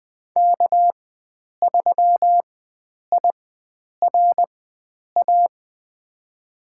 Morse code, heard as a keyed tone, about 20 wpm, spaced as for 9 wpm, 700 Hz. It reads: K3IRA
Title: USA Call Signs - 5 Characters - 4x Character Spacing - 20wpm